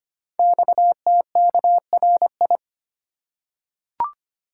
Morse code, heard as a keyed tone, about 25 wpm, 700 Hz. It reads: XTKRI
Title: Code Groups - 5 Letters - 25wpm